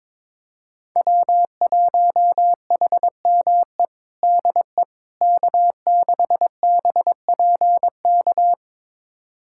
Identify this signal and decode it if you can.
Morse code, heard as a keyed tone, about 22 wpm, 700 Hz. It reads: W1HME DE K6BPK
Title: USA Callsign DE Callsign - 5 Characters - 22wpm